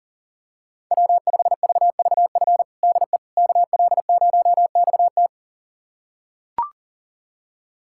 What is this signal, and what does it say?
Morse code, heard as a keyed tone, about 40 wpm, 700 Hz. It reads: W5VVF DE KL0XT
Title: USA Callsign DE Callsign - 5 Characters - 40wpm